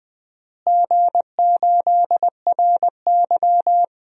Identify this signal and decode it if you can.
Morse code, heard as a keyed tone, about 21 wpm, 700 Hz. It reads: G8RY
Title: DXCC-100 Call Signs - 4 Characters - Repeated 3x - 20wpm